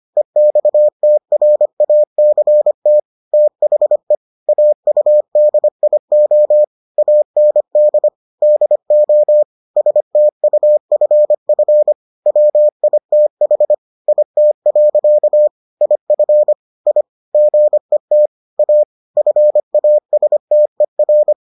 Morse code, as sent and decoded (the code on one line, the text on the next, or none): . -..- - .-. .- -.-. - / - .... / .- ..- -.. .. --- / .- -. -.. / -.. --- / ... - ..- ..-. ..-. / .-- .. - .... / .. - .-.-.- / .. ..-. / .. / --. - / .- / ..-. .- ... - . .-.
EXTRACT THE AUDIO AND DO STUFF WITH IT. IF I GET A FASTER